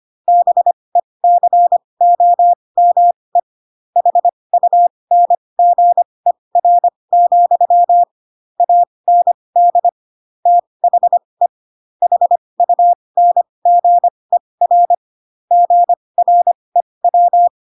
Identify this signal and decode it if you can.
Morse code, heard as a keyed tone, about 25 wpm, spaced as for 20 wpm, 700 Hz. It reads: BECOME HUNGER, AND THE HUNGER GREW